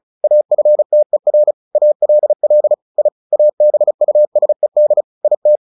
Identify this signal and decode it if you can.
Morse code, heard as a keyed tone, about 34 wpm, 600 Hz. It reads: AFTER ALL I ABUSED IT
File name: the old man and the sea / the old man and the sea 1 35wpm